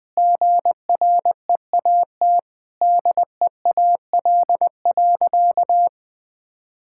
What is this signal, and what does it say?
Morse code, heard as a keyed tone, about 21 wpm, 700 Hz. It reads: GREAT DEAL.
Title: Sentences with 3-Word Phrases - Spoken Beginning - 20wpm